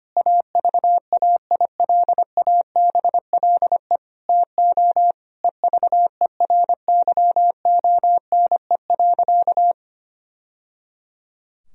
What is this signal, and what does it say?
Morse code, heard as a keyed tone, about 25 wpm, 700 Hz. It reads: AVAILABLE TO EVERYONE.
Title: Sentences from Top 1000 Words - Review Part 3 - 25wpm